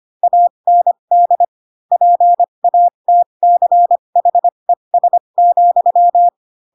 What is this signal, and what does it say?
Morse code, heard as a keyed tone, about 25 wpm, spaced as for 22 wpm, 700 Hz. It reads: AND PATCHES,